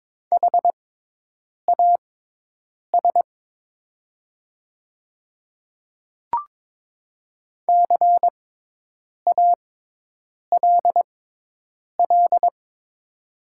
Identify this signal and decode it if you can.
Morse code, heard as a keyed tone, about 22 wpm, spaced as for 8 wpm, 700 Hz. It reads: HAS CALL
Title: 6x Character Spacing - Top 100 Words - 22wpm